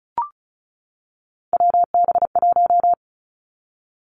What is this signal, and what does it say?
Morse code, heard as a keyed tone, about 35 wpm, 700 Hz. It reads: WB1
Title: Three Challenging Letter-Number Combinations 35wpm